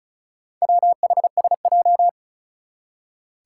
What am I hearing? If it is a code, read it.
Morse code, heard as a keyed tone, about 36 wpm, 700 Hz. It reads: WHSJ
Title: Four Challenging Letter-Number Combinations 35wpm